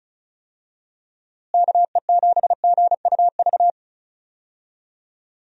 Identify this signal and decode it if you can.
Morse code, heard as a keyed tone, about 36 wpm, 700 Hz. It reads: KE7GUV
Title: USA Call Signs - 6 Characters - Repeated 2x - 35wpm